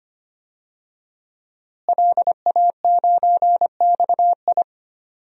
Morse code, heard as a keyed tone, about 25 wpm, 700 Hz. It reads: LA9XI
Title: DXCC-100 Call Signs - 5 Characters - Repeated 3x - 25wpm